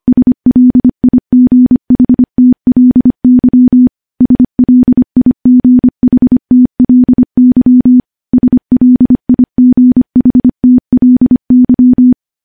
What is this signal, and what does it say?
Morse code, heard as a keyed tone, about 25 wpm, 250 Hz. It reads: SLIGHTLY SLIGHTLY SLIGHTLY
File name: morse-training-25WPM